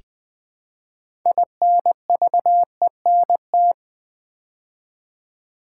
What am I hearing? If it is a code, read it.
Morse code, heard as a keyed tone, about 20 wpm, 700 Hz. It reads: INVENT
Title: Build A Word - 3 Letters At A Time - 9-Letter Words - 20wpm